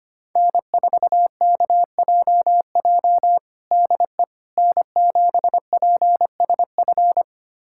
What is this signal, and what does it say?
Morse code, heard as a keyed tone, about 25 wpm, 700 Hz. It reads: N4KJJ DE N7PSF
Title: USA Callsign DE Callsign - 5 Characters - 25wpm